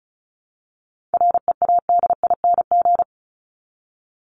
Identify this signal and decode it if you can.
Morse code, heard as a keyed tone, about 34 wpm, 700 Hz. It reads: READING
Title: Top 501-750 Words 35wpm